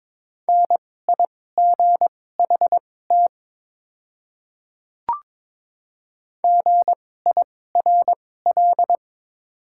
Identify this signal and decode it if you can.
Morse code, heard as a keyed tone, about 22 wpm, spaced as for 16 wpm, 690 Hz. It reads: NIGHT GIRL
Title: 2x Character Spacing - Top 500 Words - 22wpm